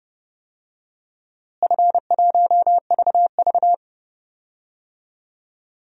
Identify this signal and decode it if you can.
Morse code, heard as a keyed tone, about 31 wpm, 700 Hz. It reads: F1VV
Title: DXCC-100 Call Signs - 4 Characters - Repeated 3x - 30wpm